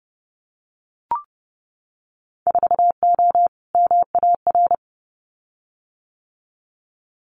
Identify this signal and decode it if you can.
Morse code, heard as a keyed tone, about 30 wpm, 700 Hz. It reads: 4O MAR
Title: Winter Field Day Exchanges Encore 30wpm